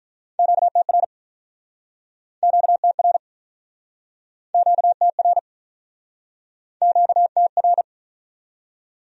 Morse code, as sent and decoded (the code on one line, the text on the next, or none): --.- - .-. / --.- - .-. / --.- - .-. / --.- - .-.
QTR QTR QTR QTR